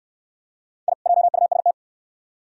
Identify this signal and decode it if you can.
Morse code, heard as a keyed tone, about 61 wpm, 700 Hz. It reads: I JUST